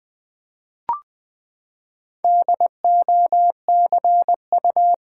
Morse code, heard as a keyed tone, about 20 wpm, 700 Hz. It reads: DOCU
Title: Build A Word - 4 Letters At A Time - 11-Letter Words - 20wpm